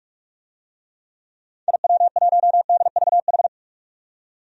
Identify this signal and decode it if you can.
Morse code, heard as a keyed tone, about 41 wpm, 700 Hz. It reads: IW1DVH
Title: DXCC-100 Call Signs - 6 Characters - 45wpm